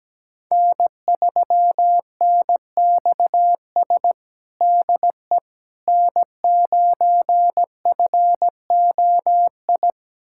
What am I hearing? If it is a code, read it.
Morse code, heard as a keyed tone, about 17 wpm, 700 Hz. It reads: N3NXS DE N9FOI
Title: USA Callsign DE Callsign - 5 Characters - 17wpm